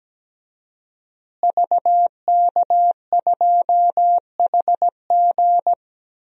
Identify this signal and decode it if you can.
Morse code, heard as a keyed tone, about 17 wpm, 700 Hz. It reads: VK2HG
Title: DXCC-100 Call Signs - 5 Characters - Repeated 3x - 17wpm